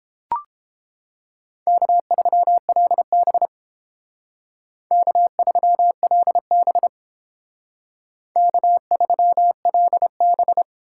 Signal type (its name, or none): Morse code